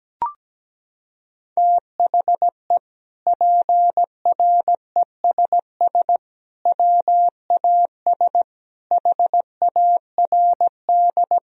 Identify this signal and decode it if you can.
Morse code, heard as a keyed tone, about 17 wpm, 700 Hz. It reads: THE PRESS WAS HARD
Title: Sentences from Top 1000 Words - Review Part 1 - 17wpm